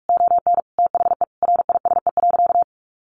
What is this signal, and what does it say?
Morse code, heard as a keyed tone, about 44 wpm, 700 Hz. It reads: ON THE RISE.
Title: Sentences from Top 1300 Words 45wpm